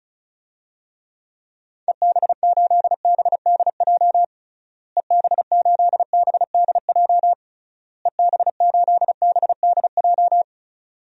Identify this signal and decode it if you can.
Morse code, heard as a keyed tone, about 34 wpm, 700 Hz. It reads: EB8BDJ EB8BDJ EB8BDJ
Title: DXCC-100 Call Signs - 6 Characters - Repeated 3x - 35wpm